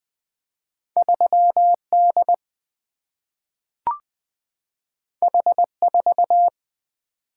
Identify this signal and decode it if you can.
Morse code, heard as a keyed tone, about 20 wpm, 700 Hz. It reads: 3D H4
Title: Single-Letter Single-Number 20wpm